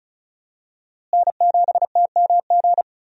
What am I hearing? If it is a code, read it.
Morse code, heard as a keyed tone, about 36 wpm, 700 Hz. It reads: N7TMG
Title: USA Call Signs - 5 Characters - Repeated 3x - 35wpm